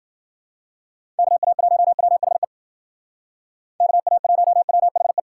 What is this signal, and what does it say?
Morse code, heard as a keyed tone, about 52 wpm, 700 Hz. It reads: KA1WHE KA1WHE